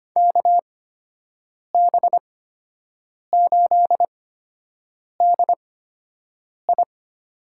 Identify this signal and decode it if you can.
Morse code, heard as a keyed tone, about 25 wpm, spaced as for 7 wpm, 700 Hz. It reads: KB8DI
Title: USA Call Signs - 6 Characters - 8x Character Spacing - 25wpm